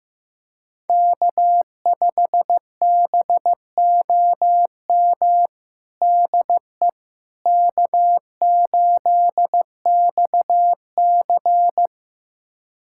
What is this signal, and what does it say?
Morse code, heard as a keyed tone, about 15 wpm, 700 Hz. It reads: K5BOM DE K8XC
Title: USA Callsign DE Callsign 15wpm